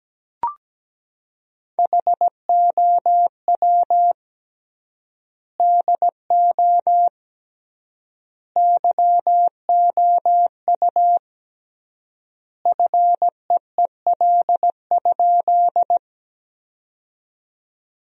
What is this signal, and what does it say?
Morse code, heard as a keyed tone, about 17 wpm, 700 Hz. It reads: HOW DO YOU FEEL?
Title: 3x Word-Spacing - Sentences from Top 400 Words - 17wpm